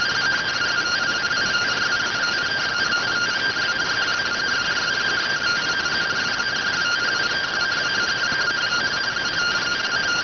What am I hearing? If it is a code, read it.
Two-tone teleprinter signal, about 170 Hz shift.